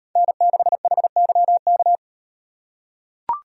Morse code, heard as a keyed tone, about 36 wpm, 700 Hz. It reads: N6HYK